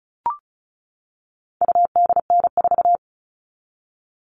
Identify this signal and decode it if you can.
Morse code, heard as a keyed tone, about 34 wpm, 700 Hz. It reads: UDN4
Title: All Cut Numbers - 4-Digit 35wpm